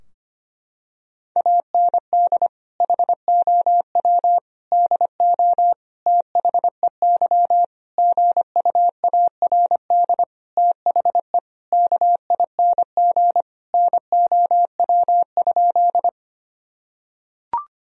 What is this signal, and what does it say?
Morse code, heard as a keyed tone, about 25 wpm, 700 Hz. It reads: AND HOW DO THEY GUARD THE KING NOW?
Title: Sentences from Top 1000 Words - Review Part 4 - 25wpm